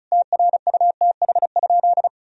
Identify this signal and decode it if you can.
Morse code, heard as a keyed tone, about 36 wpm, 700 Hz. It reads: TRUTH?
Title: Sentences from Top 1000 Words - Review Part 2 - 35wpm